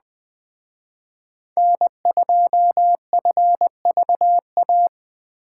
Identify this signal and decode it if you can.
Morse code, heard as a keyed tone, about 20 wpm, 700 Hz. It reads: N2FVA
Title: Speed Racing - USA 1x3 Call Signs 17wpm